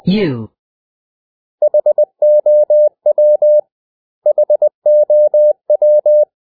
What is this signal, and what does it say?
Morse code, heard as a keyed tone, about 20 wpm, 600 Hz. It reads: HOW HOW
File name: cw / 100 Common English Words